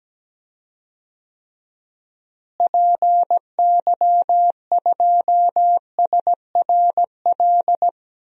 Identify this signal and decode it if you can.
Morse code, heard as a keyed tone, about 17 wpm, 700 Hz. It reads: PY2SRL
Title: DXCC-100 Call Signs - 6 Characters - Repeated 2x - 17wpm